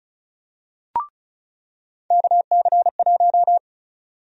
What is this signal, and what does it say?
Morse code, heard as a keyed tone, about 36 wpm, 700 Hz. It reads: KC1
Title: USA Call Sign Prefixes - 35wpm